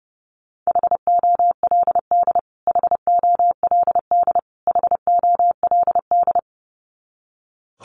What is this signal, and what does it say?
Morse code, heard as a keyed tone, about 30 wpm, 700 Hz. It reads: HOLD HOLD HOLD
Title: Top 500 Words Repeated 30wpm